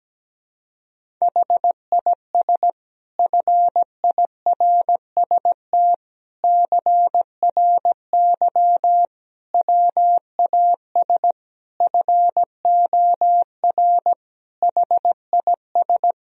Morse code, heard as a keyed tone, about 17 wpm, 700 Hz. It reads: HIS FIRST CRY WAS FOR HIS
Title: Sentences from Top 1000 Words - Review Part 2 - 17wpm